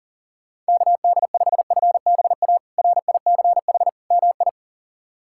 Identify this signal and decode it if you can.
Morse code, heard as a keyed tone, about 40 wpm, 700 Hz. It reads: KD5FBA RICH MI
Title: K1USN SST QSO Practice - 40wpm